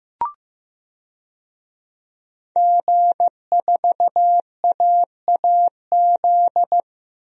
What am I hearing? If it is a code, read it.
Morse code, heard as a keyed tone, about 15 wpm, 700 Hz. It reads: G4AAZ
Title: DXCC-100 Call Signs - 5 Characters - Repeated 2x - 15wpm